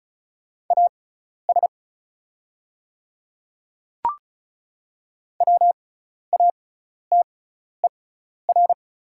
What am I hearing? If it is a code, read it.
Morse code, heard as a keyed tone, about 34 wpm, spaced as for 12 wpm, 690 Hz. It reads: AS WATER
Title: 6x Character Spacing - Top 100 Words - 35wpm